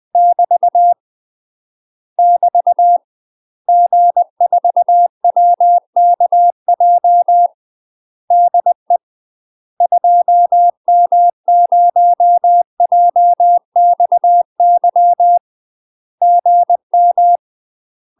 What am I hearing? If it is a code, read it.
Morse code, heard as a keyed tone, about 20 wpm, 700 Hz. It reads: <BT> <BT> G4WKJ DE 2M0JXY GM
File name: UK MASTER QSOs TRAINING / 20WPM / UK MASTER QSOs 0036